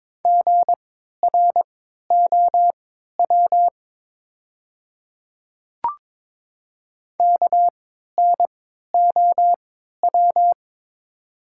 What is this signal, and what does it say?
Morse code, heard as a keyed tone, about 22 wpm, spaced as for 12 wpm, 700 Hz. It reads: GROW KNOW
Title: 3x Character Spacing - Top 300 Words - 22wpm